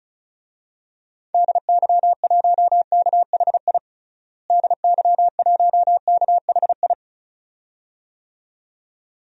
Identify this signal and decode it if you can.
Morse code, heard as a keyed tone, about 36 wpm, 700 Hz. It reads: DY1KHI DY1KHI
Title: DXCC-100 Call Signs - 6 Characters - Repeated 2x - 35wpm